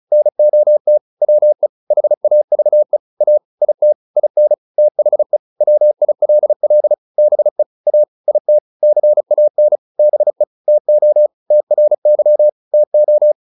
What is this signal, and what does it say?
Morse code, heard as a keyed tone, about 34 wpm, 600 Hz. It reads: NOT WE HAVE A IT IN THE WILL BE A IT CAN BE TO TRY TO
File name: WD202 35